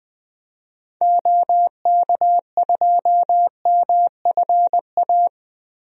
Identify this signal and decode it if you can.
Morse code, heard as a keyed tone, about 20 wpm, 700 Hz. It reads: OK2MFA
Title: DXCC-100 Call Signs - 6 Characters - Repeated 3x - 20wpm